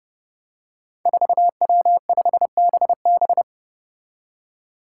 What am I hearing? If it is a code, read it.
Morse code, heard as a keyed tone, about 29 wpm, 700 Hz. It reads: 4W5BB
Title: Five Challenging Letter-Number Combinations 30wpm